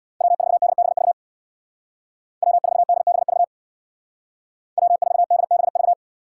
Morse code, heard as a keyed tone, about 65 wpm, 700 Hz. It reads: W4DBV W4DBV W4DBV